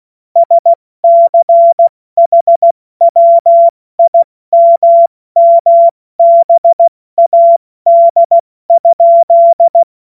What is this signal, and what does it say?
Morse code, heard as a keyed tone, about 16 wpm, 680 Hz. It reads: SCHWIMMBAD?
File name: Morsekurs / Witze / Witz 21. September 2021, 70 F80